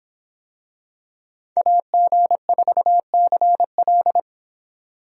Morse code, heard as a keyed tone, about 25 wpm, 700 Hz. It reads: AG4CL